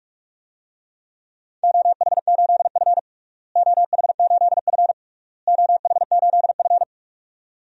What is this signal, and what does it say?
Morse code, heard as a keyed tone, about 42 wpm, 700 Hz. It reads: OH8F OH8F OH8F
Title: DXCC-100 Call Signs - 4 Characters - Repeated 3x - 45wpm